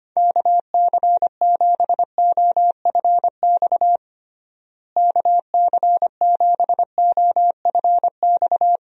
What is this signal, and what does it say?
Morse code, heard as a keyed tone, about 25 wpm, 700 Hz. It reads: KC7OFX KC7OFX
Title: USA Call Signs - 6 Characters - Repeated 3x - 25wpm